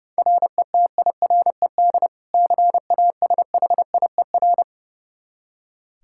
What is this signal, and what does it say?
Morse code, heard as a keyed tone, about 30 wpm, 700 Hz. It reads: RETIRED CASHIER
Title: K1USN SST QSO Practice 30wpm